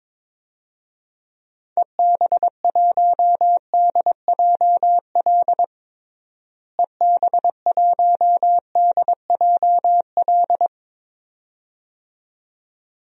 Morse code, heard as a keyed tone, about 22 wpm, 700 Hz. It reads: EB1DJL EB1DJL
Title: DXCC-100 Call Signs - 6 Characters - Repeated 2x - 22wpm